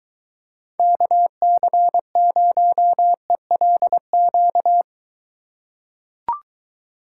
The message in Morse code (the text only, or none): KC0ELQ